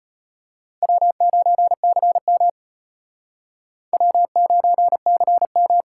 Morse code, heard as a keyed tone, about 32 wpm, 700 Hz. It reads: W9CM W9CM